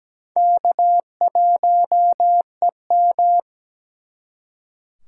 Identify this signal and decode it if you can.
Morse code, heard as a keyed tone, about 20 wpm, 700 Hz. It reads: K1EM
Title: Speed Racing - USA 1x2 Call Signs 17wpm